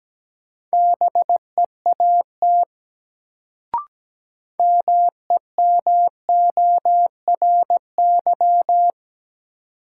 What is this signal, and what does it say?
Morse code, heard as a keyed tone, about 17 wpm, 700 Hz. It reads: BEAT MEMORY
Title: Sentences from Top 1300 Words - New Word Drill - 17wpm